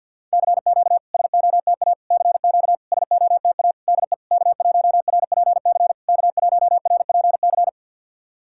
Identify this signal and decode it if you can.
Morse code, heard as a keyed tone, about 50 wpm, 700 Hz. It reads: CQ SOTA CQ SOTA DE K1RPC K1RPC
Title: SOTA QSO Practice 50wpm